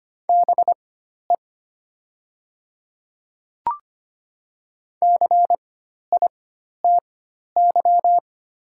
Morse code, heard as a keyed tone, about 25 wpm, spaced as for 12 wpm, 700 Hz. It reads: BE CITY
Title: 4x Character Spacing - Top 300 Words - 25wpm